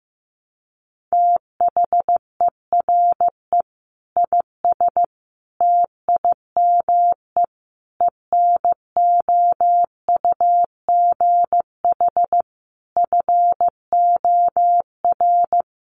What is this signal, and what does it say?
Morse code, heard as a keyed tone, about 15 wpm, 700 Hz. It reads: THERE IS TIME ENOUGH FOR
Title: Sentences from Top 1200 Words 15wpm